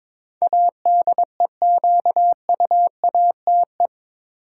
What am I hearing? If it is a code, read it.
Morse code, heard as a keyed tone, about 22 wpm, 700 Hz. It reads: ADEQUATE
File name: practice / 22wpm / Eight Letter Words - 22wpm